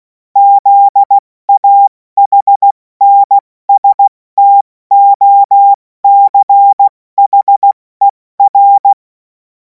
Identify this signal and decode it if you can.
Morse code, heard as a keyed tone, about 16 wpm, spaced as for 14 wpm, 800 Hz. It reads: ZAHNSTOCHER
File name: Morsekurs / Witze / Witz 17. März 2026 70 F80